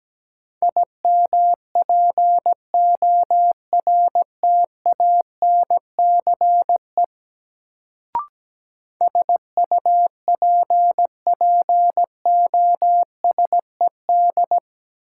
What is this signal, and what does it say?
Morse code, heard as a keyed tone, about 17 wpm, 700 Hz. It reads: IMPORTANCE SUPPOSED
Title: Sets of 3 Words - New Word Drill - 17wpm